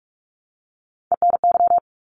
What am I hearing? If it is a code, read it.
Morse code, heard as a keyed tone, about 46 wpm, 700 Hz. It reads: ENY
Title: ARRL Sections 45wpm